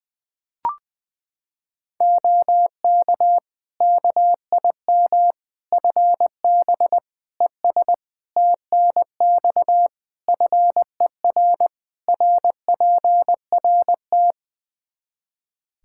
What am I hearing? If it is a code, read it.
Morse code, heard as a keyed tone, about 20 wpm, 700 Hz. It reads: OK KIM FB ES TNX FER RPRT
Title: Practice QSO - Exchange 2 - Info - 20wpm